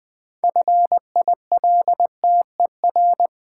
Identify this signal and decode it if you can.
Morse code, heard as a keyed tone, about 20 wpm, 700 Hz. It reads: FILTER
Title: 6 Letter Words - First 3 Letters Given - 20wpm